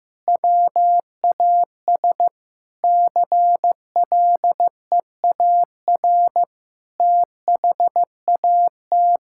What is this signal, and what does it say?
Morse code, heard as a keyed tone, about 15 wpm, 700 Hz. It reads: WAS CLEAR THAT
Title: Sentences from Top 1000 Words - Review Part 6 - 15wpm